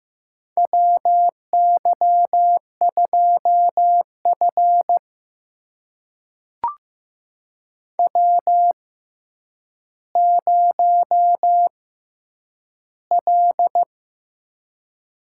Morse code, heard as a keyed tone, about 15 wpm, spaced as for 8 wpm, 700 Hz. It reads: WY2F W0L
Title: USA Call Signs - 6x Character Spacing - 15wpm